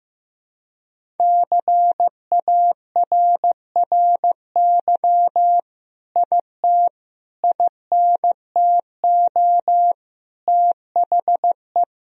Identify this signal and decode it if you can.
Morse code, heard as a keyed tone, about 15 wpm, 700 Hz. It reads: CARRY IT INTO THE
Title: Sentences from Top 1000 Words - Review Part 1 - 15wpm